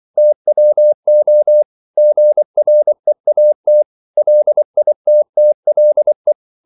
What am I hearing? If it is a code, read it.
Morse code, heard as a keyed tone, about 24 wpm, 600 Hz. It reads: TWO GREAT LITTLE
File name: CW / Practice / 500MostCommonEnglishWords24